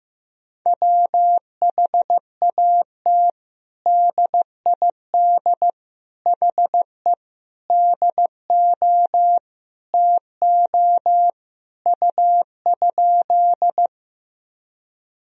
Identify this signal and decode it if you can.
Morse code, heard as a keyed tone, about 15 wpm, 700 Hz. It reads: WHAT DID HE DO TO U?
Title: Sentences from Top 100 Words and Abbreviations 15wpm